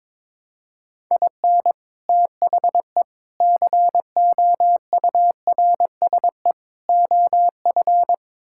Morse code, heard as a keyed tone, about 22 wpm, 700 Hz. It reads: IN THE COURSE OF
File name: practice / 22wpm / Build A Phrase - 5-Word Phrases - 22wpm